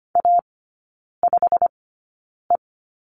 Morse code, heard as a keyed tone, about 25 wpm, spaced as for 9 wpm, 700 Hz. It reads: A5E
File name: course / 25:10wpm / 100 Lesson 17e - Letter u - Three Characters 25wpm